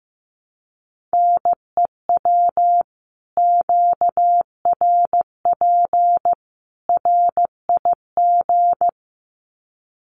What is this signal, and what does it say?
Morse code, heard as a keyed tone, about 15 wpm, 700 Hz. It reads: NEW QRP RIG
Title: Common QSO Phrases 15wpm